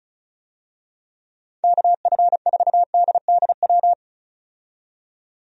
Morse code, heard as a keyed tone, about 34 wpm, 700 Hz. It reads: KF4DDW
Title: USA Call Signs - 6 Characters - 35wpm